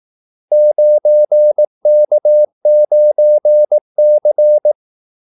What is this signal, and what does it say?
Morse code, heard as a keyed tone, about 18 wpm, 600 Hz. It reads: 9K9C
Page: Kaldesignaler 18 wpm